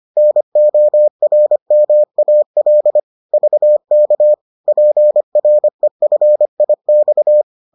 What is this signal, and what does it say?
Morse code, heard as a keyed tone, about 25 wpm, 600 Hz. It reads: NORMAL VK PREFIX